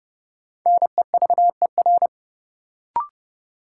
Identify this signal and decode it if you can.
Morse code, heard as a keyed tone, about 30 wpm, 700 Hz. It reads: NEVER
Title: Sentences from Top 300 Words - New Word Drill - 30wpm